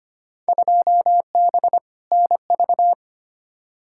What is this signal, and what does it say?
Morse code, heard as a keyed tone, about 25 wpm, 700 Hz. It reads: 2B NV
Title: Field Day Exchanges Encore - 25wpm